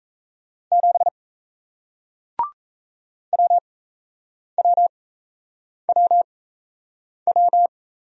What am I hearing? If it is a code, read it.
Morse code, heard as a keyed tone, about 38 wpm, spaced as for 8 wpm, 700 Hz. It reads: 7 WWWW